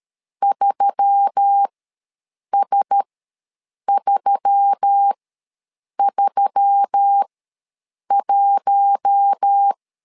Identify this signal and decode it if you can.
Morse code, heard as a keyed tone, about 13 wpm, spaced as for 7 wpm, 800 Hz. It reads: 3S331
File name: LEZIONI / 4 Q1S3 ws 12wpm 2038